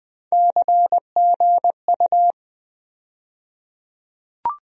Morse code, heard as a keyed tone, about 20 wpm, 700 Hz. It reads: CGU
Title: USA Call Sign Postfixes - 3 Letters 20wpm